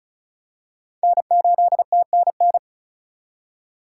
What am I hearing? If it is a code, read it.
Morse code, heard as a keyed tone, about 36 wpm, 700 Hz. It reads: N8TNN
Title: USA Call Signs - 2x Character Spacing - 35wpm